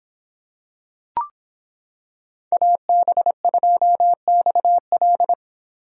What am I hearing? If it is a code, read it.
Morse code, heard as a keyed tone, about 25 wpm, 700 Hz. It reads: AB2XL